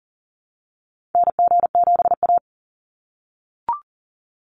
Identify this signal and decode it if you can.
Morse code, heard as a keyed tone, about 39 wpm, 700 Hz. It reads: NG7A
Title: NA SOTA Activator Callsigns 40wpm